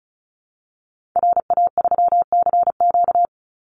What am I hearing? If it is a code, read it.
Morse code, heard as a keyed tone, about 34 wpm, 700 Hz. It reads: RA3CQ
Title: Call Signs Repeated 3x - 35wpm